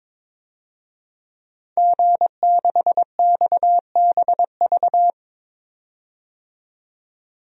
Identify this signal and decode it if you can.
Morse code, heard as a keyed tone, about 22 wpm, 700 Hz. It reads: G6XBV
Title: DXCC-100 Call Signs - 5 Characters - 22wpm